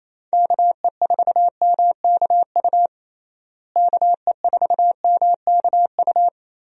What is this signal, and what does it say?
Morse code, heard as a keyed tone, about 28 wpm, 700 Hz. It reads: KE4MKU KE4MKU
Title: USA Call Signs - 6 Characters - Repeated 3x - 28wpm